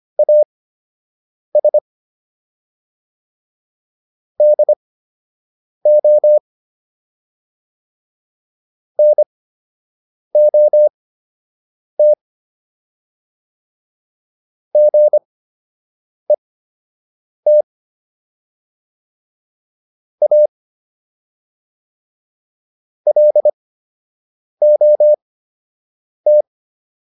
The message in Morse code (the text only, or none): AS DO NOT GET A LOT